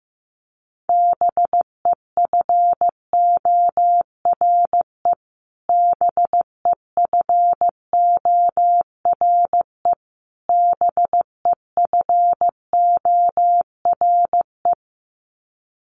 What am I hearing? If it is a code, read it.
Morse code, heard as a keyed tone, about 15 wpm, 700 Hz. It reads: BEFORE BEFORE BEFORE
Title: Top 500 Words Repeated 15wpm